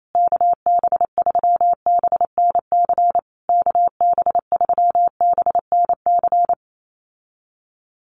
Morse code, heard as a keyed tone, about 28 wpm, 700 Hz. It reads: KB3BNC KB3BNC
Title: USA Call Signs Repeated 3x 28wpm